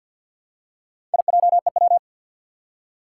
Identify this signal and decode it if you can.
Morse code, heard as a keyed tone, about 50 wpm, 700 Hz. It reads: IJEW
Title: Four Challenging Letter-Number Combinations 50wpm